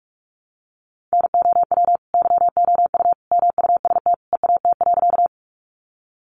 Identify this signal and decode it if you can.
Morse code, heard as a keyed tone, about 44 wpm, 700 Hz. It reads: NOW YOU MUST EAT.
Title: Sentences from Top 300 Words 45wpm